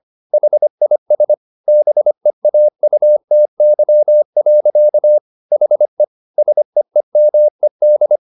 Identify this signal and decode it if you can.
Morse code, heard as a keyed tone, about 25 wpm, 600 Hz. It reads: HIS BEAUTY. HE SEEMED